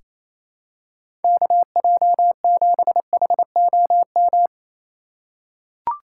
Morse code, heard as a keyed tone, about 28 wpm, 700 Hz. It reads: KJ7HOM
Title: USA Call Signs - 3x Character Spacing - 28wpm